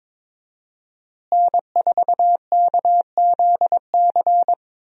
Morse code, heard as a keyed tone, about 22 wpm, 700 Hz. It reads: N4KZC
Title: USA Call Signs - 5 Characters - Repeated 3x - 22wpm